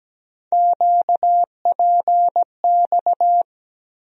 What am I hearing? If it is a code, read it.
Morse code, heard as a keyed tone, about 17 wpm, 700 Hz. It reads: QPX
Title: USA Call Sign Postfixes - 3 Letters 17wpm